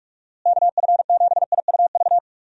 Morse code, heard as a keyed tone, about 46 wpm, 700 Hz. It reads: KF7IUV
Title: USA Call Signs - 6 Characters - Repeated 2x - 45wpm